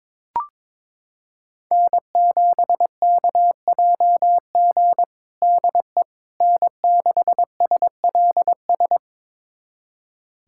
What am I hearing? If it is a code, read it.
Morse code, heard as a keyed tone, about 22 wpm, 700 Hz. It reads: N7KJG DE N6SLS
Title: USA Callsign DE Callsign - 5 Characters - 22wpm